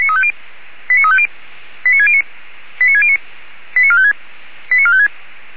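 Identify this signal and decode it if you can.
Five-tone selective call (ZVEI), heard as quick runs of five tones, tones of about 70 ms. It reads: 89350 89609 89609 89456 89456
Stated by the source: ZVEI five-tone call received on an SDR, from the feuerwehr-orlamuende.org (fire brigade Orlamünde) recording server.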